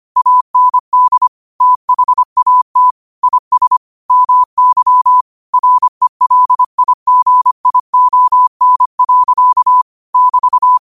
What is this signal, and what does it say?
Morse code, heard as a keyed tone, about 25 wpm, 1000 Hz. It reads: AND THAT IS MY RELIGION. <BT>
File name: qotdfiles / 25wpm